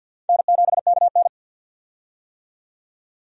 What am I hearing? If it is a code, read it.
Morse code, heard as a keyed tone, about 47 wpm, 700 Hz. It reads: N7KN
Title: USA Call Signs - 4 Characters - 50wpm